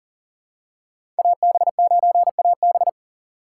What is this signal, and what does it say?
Morse code, heard as a keyed tone, about 40 wpm, 700 Hz. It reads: AB9AB
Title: USA Call Signs - 5 Characters - 2x Character Spacing - 40wpm